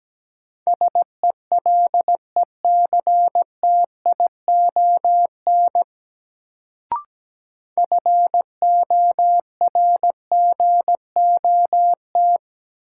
Morse code, heard as a keyed tone, about 17 wpm, 700 Hz. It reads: SELECTION FORGOT
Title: Sentences from Top 1400 Words - New Word Drill Alt - 17wpm